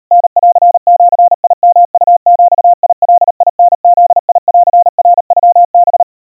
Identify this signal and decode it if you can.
Morse code (Japanese Wabun code), heard as a keyed tone, about 39 wpm, 700 Hz. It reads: タンジヨウビガタブンナノハ
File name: kawa / Morse sample / kamyu 140 cpm 700 Hz Crispy